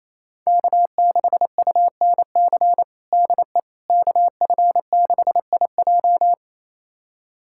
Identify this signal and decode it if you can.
Morse code, heard as a keyed tone, about 27 wpm, 700 Hz. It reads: K6UNC DE KF6IJ
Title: USA Callsign DE Callsign - 5 Characters - 28wpm